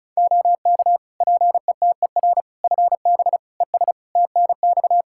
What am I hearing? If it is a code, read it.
Morse code, heard as a keyed tone, about 34 wpm, 700 Hz. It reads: OK PETER FB ES TNX
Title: K1USN SST QSO Practice 35wpm